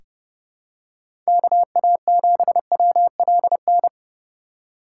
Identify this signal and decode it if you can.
Morse code, heard as a keyed tone, about 30 wpm, 700 Hz. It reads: KA7WLN
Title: USA Call Signs - 6 Characters - 4x Character Spacing - 30wpm